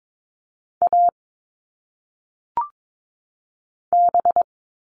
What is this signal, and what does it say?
Morse code, heard as a keyed tone, about 22 wpm, 700 Hz. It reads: A B
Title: All Cut Numbers - 1-Digit 22wpm